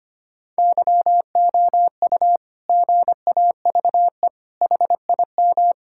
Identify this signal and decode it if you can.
Morse code, heard as a keyed tone, about 25 wpm, 700 Hz. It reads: YOU GAVE HIM
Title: Sentences from Top 1000 Words - Review Part 6 - 25wpm